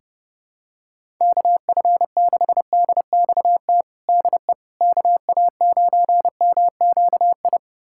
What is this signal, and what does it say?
Morse code, heard as a keyed tone, about 32 wpm, 700 Hz. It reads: KF6DXT DE KA9MQI
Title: USA Callsign DE Callsign - 6 Characters - 30wpm